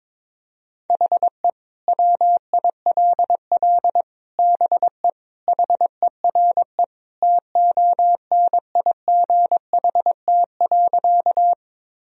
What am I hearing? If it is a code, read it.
Morse code, heard as a keyed tone, about 22 wpm, 700 Hz. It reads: HE WILL BE HERE TONIGHT.
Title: Sentences from Top 1000 Words - Review Part 3 - 22wpm